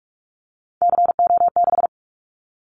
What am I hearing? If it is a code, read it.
Morse code, heard as a keyed tone, about 44 wpm, 700 Hz. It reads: CO6
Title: Callsign Prefixes for Top 100 DXCC - 45wpm